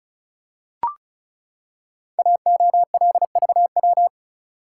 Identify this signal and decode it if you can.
Morse code, heard as a keyed tone, about 36 wpm, 700 Hz. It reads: AOLVW
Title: Code Groups - 5 Letters - 35wpm